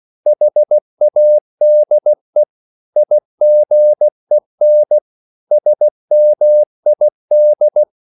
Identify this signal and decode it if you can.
Morse code, heard as a keyed tone, about 16 wpm, 600 Hz. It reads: HADE IGEN SMID
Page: Korte danske ord 16wpm